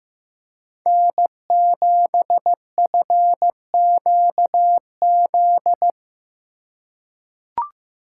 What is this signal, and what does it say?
Morse code, heard as a keyed tone, about 15 wpm, 700 Hz. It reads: N7FQZ
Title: USA Call Signs - 4x Character Spacing - 15wpm